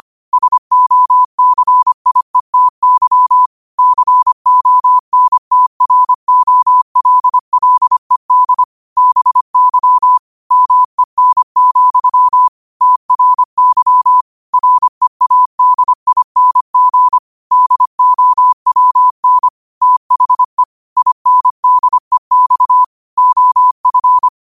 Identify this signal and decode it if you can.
Morse code, heard as a keyed tone, about 25 wpm, 1000 Hz. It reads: SOCIETY CONTROLLED BY MEN, TRY READING DOWN THE INDEX OF
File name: qotdfiles / 25wpm